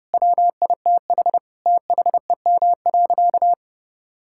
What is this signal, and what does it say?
Morse code, heard as a keyed tone, about 29 wpm, 700 Hz. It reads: WITH THEM.
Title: Sentences from Top 1000 Words - Review Part 6 - 30wpm